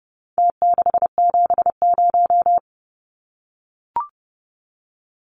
Morse code, heard as a keyed tone, about 30 wpm, 700 Hz. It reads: T670
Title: Numbers - Contest Style - 4-Digit 30wpm